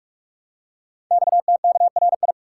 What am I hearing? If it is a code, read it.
Morse code, heard as a keyed tone, about 44 wpm, 700 Hz. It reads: XTKRI
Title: Code Groups - 5 Letters - 45wpm